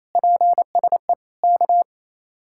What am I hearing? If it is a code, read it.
Morse code, heard as a keyed tone, about 35 wpm, 700 Hz. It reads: PSE K